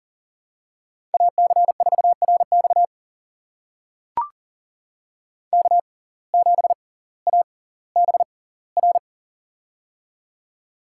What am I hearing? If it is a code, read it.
Morse code, heard as a keyed tone, about 40 wpm, spaced as for 20 wpm, 700 Hz. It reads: AC4RX K7ABR
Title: USA Call Signs - 5 Characters - 6x Character Spacing - 40wpm